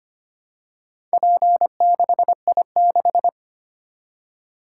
Morse code, heard as a keyed tone, about 25 wpm, 700 Hz. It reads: P6I6
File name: practice / 25wpm / Four Letter-Number Combinations - 25wpm